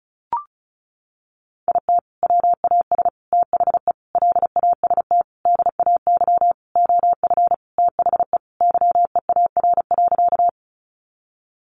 Morse code, heard as a keyed tone, about 34 wpm, 700 Hz. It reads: IT WAS THE LAST DAY OF THE YEAR.
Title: Sentences from Top 300 Words 35wpm